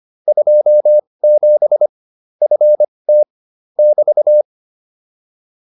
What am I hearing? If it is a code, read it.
Morse code, heard as a keyed tone, about 25 wpm, spaced as for 20 wpm, 600 Hz. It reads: 27 FT <BT>